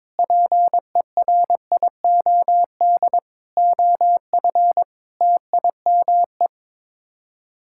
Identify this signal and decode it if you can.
Morse code, heard as a keyed tone, about 22 wpm, 700 Hz. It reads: PERIOD OF TIME
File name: practice / 22wpm / Sets of 7 Words - 22wpm